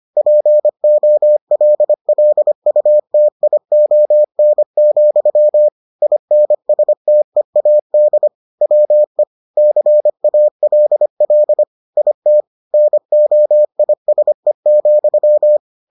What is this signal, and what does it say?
Morse code, heard as a keyed tone, about 25 wpm, 600 Hz. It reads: POLLUTION, INSTEAD WE CALL IT NOISE,